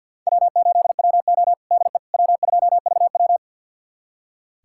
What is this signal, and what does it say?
Morse code, heard as a keyed tone, about 46 wpm, 690 Hz. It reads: W8WO DE W2VW
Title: USA Callsign DE Callsign - 4 Characters - 50wpm